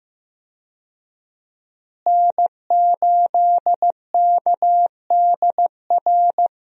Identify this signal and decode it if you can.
Morse code, heard as a keyed tone, about 15 wpm, 700 Hz. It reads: N8KDR
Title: USA Call Signs - 5 Characters - Repeated 3x - 15wpm